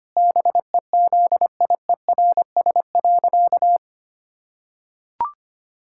Morse code, heard as a keyed tone, about 25 wpm, 700 Hz. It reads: BEZIERS.
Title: Pangram Challenge v2 - 25wpm